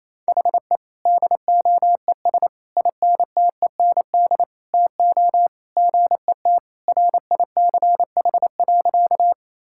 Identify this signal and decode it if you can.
Morse code, heard as a keyed tone, about 28 wpm, 700 Hz. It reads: HE DOES INTEND TO GET RICH.
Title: Sentences from Top 1600 Words 28wpm